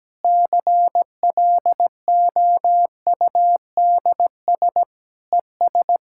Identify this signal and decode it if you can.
Morse code, heard as a keyed tone, about 17 wpm, 700 Hz. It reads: CLOUDS ES
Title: K1USN SST QSO Practice 17wpm